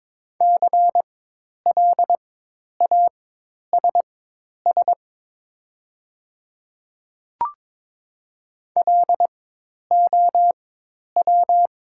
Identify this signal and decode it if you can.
Morse code, heard as a keyed tone, about 22 wpm, spaced as for 10 wpm, 700 Hz. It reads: CLASS LOW
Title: 4x Character Spacing - Top 500 Words - 22wpm